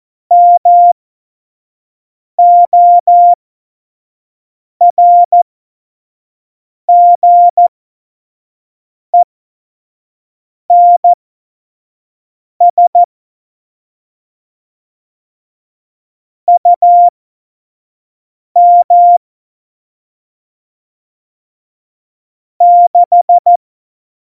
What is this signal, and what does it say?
Morse code, heard as a keyed tone, about 14 wpm, spaced as for 5 wpm, 700 Hz. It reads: MORGENS UM 6
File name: Morsekurs / Witze / Witz 23. Oktober 2018, 25 F70